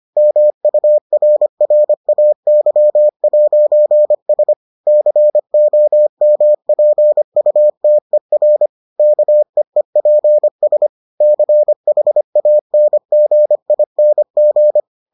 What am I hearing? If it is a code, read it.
Morse code, heard as a keyed tone, about 25 wpm, 600 Hz. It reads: MURRAY'S COMPUTER KEEPS CHANGING